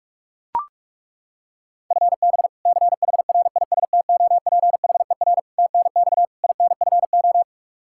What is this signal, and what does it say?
Morse code, heard as a keyed tone, about 41 wpm, 700 Hz. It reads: FB CHRISTOPHER TNX INFO
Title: K1USN SST QSO Practice 45wpm